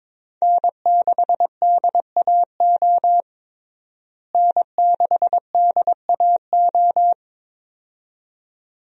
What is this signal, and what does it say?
Morse code, heard as a keyed tone, about 22 wpm, 700 Hz. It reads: N6DAO N6DAO
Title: USA Call Signs - 5 Characters - Repeated 2x - 22wpm